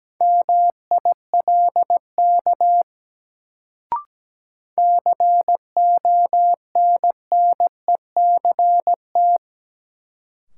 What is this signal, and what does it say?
Morse code, heard as a keyed tone, about 17 wpm, 700 Hz. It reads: MILK CONNECT
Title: Sentences from Top 1500 Words - New Word Drill Alt - 17wpm